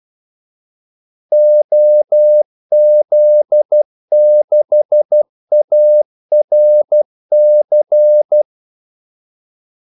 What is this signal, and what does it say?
Morse code, heard as a keyed tone, about 12 wpm, 600 Hz. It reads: OZ6ARC
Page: 2 ekstra ordmellemrum mellem kaldesignalerne